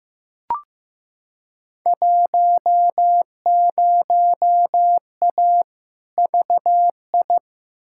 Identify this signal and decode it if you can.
Morse code, heard as a keyed tone, about 15 wpm, 700 Hz. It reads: 10A VI
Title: Field Day Exchanges Encore - 15wpm